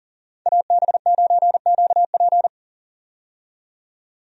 Morse code, heard as a keyed tone, about 40 wpm, 700 Hz. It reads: AB9QP
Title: USA Call Signs - 5 Characters - 40wpm